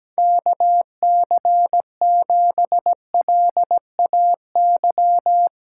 Morse code, heard as a keyed tone, about 17 wpm, 700 Hz. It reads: KC7LAY
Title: USA Callsign DE Callsign - 6 Characters - 17wpm